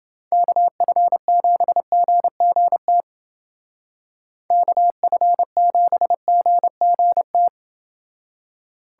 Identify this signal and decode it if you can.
Morse code, heard as a keyed tone, about 28 wpm, 700 Hz. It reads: KF7GGT KF7GGT